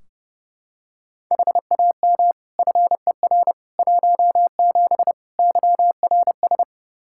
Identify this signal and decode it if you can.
Morse code, heard as a keyed tone, about 29 wpm, 700 Hz. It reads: HAM FER 17 YRS
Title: Practice QSO - Exchange 3 - More Info - 30wpm